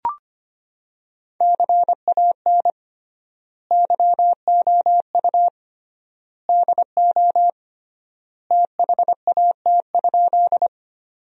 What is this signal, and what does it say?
Morse code, heard as a keyed tone, about 25 wpm, 700 Hz. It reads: CAN YOU DO THAT?
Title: 3x Word-Spacing - Sentences from Top 100 Words - 25wpm